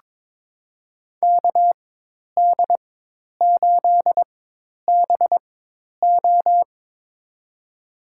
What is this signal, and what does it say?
Morse code, heard as a keyed tone, about 22 wpm, spaced as for 10 wpm, 700 Hz. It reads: KD8BO
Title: USA Call Signs - 5 Characters - 4x Character Spacing - 22wpm